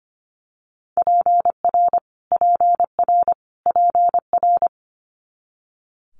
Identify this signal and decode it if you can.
Morse code, heard as a keyed tone, about 25 wpm, 700 Hz. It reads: PR PR PR
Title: Multi-letter Phonemes Repeated 3x 25wpm